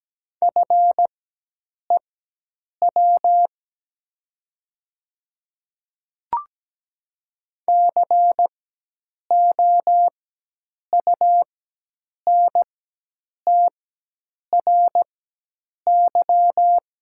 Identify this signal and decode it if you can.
Morse code, heard as a keyed tone, about 17 wpm, spaced as for 8 wpm, 700 Hz. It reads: FEW COUNTRY
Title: 4x Character Spacing - Top 300 Words - 17wpm